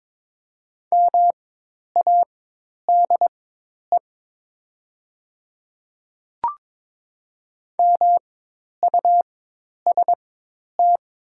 Morse code, heard as a keyed tone, about 22 wpm, spaced as for 10 wpm, 700 Hz. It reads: MADE MUST
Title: 4x Character Spacing - Top 300 Words - 22wpm